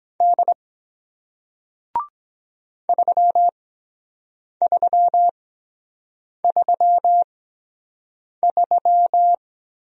Morse code, heard as a keyed tone, about 23 wpm, spaced as for 7 wpm, 700 Hz. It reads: D 3333